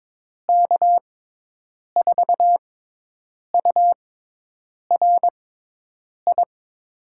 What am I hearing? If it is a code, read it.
Morse code, heard as a keyed tone, about 22 wpm, spaced as for 8 wpm, 700 Hz. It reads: K4URI
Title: USA Call Signs - 6x Character Spacing - 22wpm